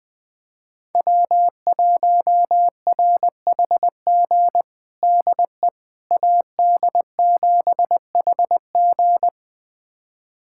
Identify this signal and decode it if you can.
Morse code, heard as a keyed tone, about 20 wpm, 700 Hz. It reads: W1RHG DE AD7HG
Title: USA Callsign DE Callsign - 5 Characters - 20wpm